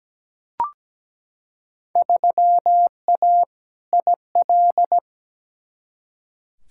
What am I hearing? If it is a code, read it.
Morse code, heard as a keyed tone, about 17 wpm, 700 Hz. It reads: 3A IL
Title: Field Day Exchanges - 17wpm